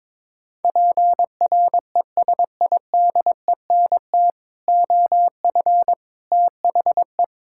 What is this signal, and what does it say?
Morse code, heard as a keyed tone, about 22 wpm, 700 Hz. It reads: PRESIDENT OF THE
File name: practice / 22wpm / Build A Phrase - 5-Word Phrases - 22wpm